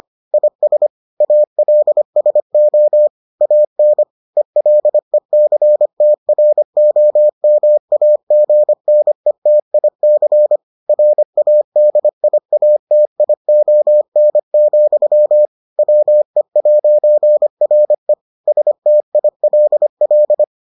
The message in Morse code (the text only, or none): IS ALSO AN ELECTROMAGNETIC RADIATION, WE'RE STILL